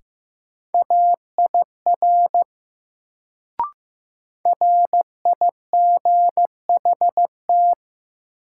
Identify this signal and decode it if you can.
Morse code, heard as a keyed tone, about 15 wpm, 700 Hz. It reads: AIR RIGHT
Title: Sentences from Top 200 Words - New Word Drill - 15wpm